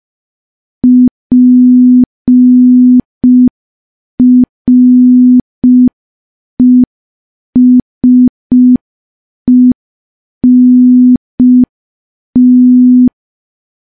Morse code, heard as a keyed tone, about 5 wpm, 250 Hz. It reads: PRESENT
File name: morse / morse-training-05WPM